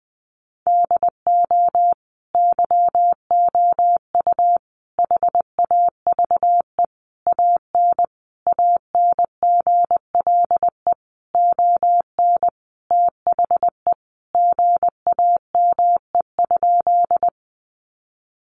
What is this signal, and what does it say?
Morse code, heard as a keyed tone, about 20 wpm, 700 Hz. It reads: DO YOU HAVE AN ANGLE ON THE GAME?
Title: Sentences from Top 1500 Words 20wpm